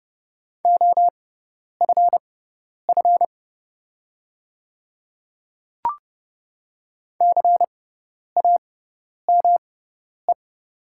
Morse code, heard as a keyed tone, about 30 wpm, spaced as for 10 wpm, 700 Hz. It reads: OFF CAME
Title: 6x Character Spacing - Top 300 Words - 30wpm